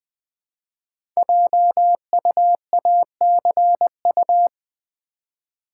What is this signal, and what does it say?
Morse code, heard as a keyed tone, about 20 wpm, 700 Hz. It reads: JUACU
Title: Code Groups - 5 Letters - 20wpm